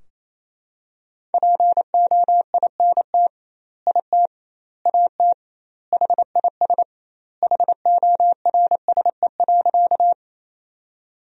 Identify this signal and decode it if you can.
Morse code, heard as a keyed tone, about 28 wpm, 700 Hz. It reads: POINT IT AT HIS HORSE.
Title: 2x Word-Spacing - Sentences from Top 300 Words - 28wpm